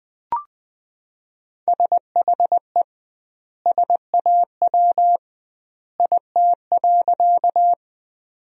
Morse code, heard as a keyed tone, about 21 wpm, 700 Hz. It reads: SHE SAW IT.
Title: 2x Word-Spacing - Sentences from Top 300 Words - 20wpm